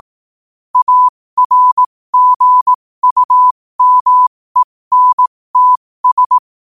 Morse code, heard as a keyed tone, about 18 wpm, spaced as for 15 wpm, 1000 Hz. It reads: ARGUMENTS
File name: qotdfiles / 15wpm